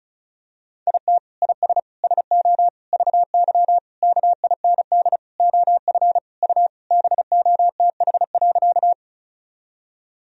Morse code, heard as a keyed tone, about 36 wpm, 700 Hz. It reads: IT IS SO VY KIND OF U BOTH.
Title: Sentences from Top 300 Words and Abbreviations 35wpm